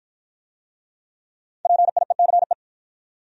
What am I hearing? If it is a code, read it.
Morse code, heard as a keyed tone, about 53 wpm, 700 Hz. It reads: PIECE